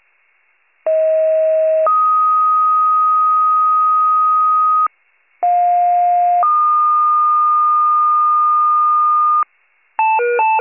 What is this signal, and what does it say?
Two-tone page, heard as a one-second tone, then a three-second tone, about 640 then 1250 Hz, about 710 then 1220 Hz. It is heard from 0.9 to 9.4 s.